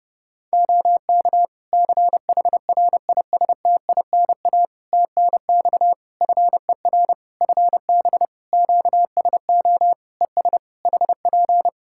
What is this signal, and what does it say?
Morse code, heard as a keyed tone, about 31 wpm, 700 Hz. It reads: OK CHRISTINA TNX FER FB QSO ES HP
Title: Practice QSO - Exchange 4 - Ending - 30wpm